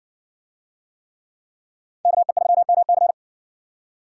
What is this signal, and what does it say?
Morse code, heard as a keyed tone, about 56 wpm, 700 Hz. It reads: KE3MC